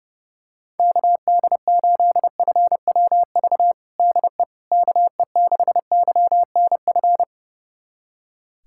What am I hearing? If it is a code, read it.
Morse code, heard as a keyed tone, about 32 wpm, 700 Hz. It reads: KD8FWV DE KE6YNF
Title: USA Callsign DE Callsign 30wpm